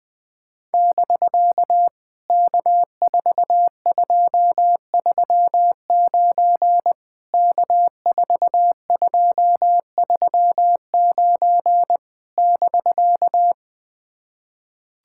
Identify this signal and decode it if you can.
Morse code, heard as a keyed tone, about 19 wpm, 700 Hz. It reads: <BK> K4239 K4239 <BK>
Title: POTA QSO Practice - 20wpm